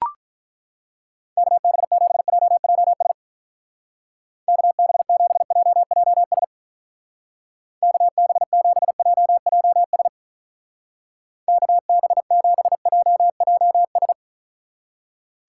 Morse code, heard as a keyed tone, about 42 wpm, 700 Hz. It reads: KB7JJS KB7JJS KB7JJS KB7JJS
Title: Speed Racing - USA 2x3 Call Signs 35wpm